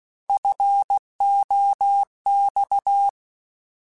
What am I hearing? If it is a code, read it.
Morse code, heard as a keyed tone, about 16 wpm, 800 Hz. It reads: FOX
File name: Alphabetical / Alphabetical mp3s / Fs @15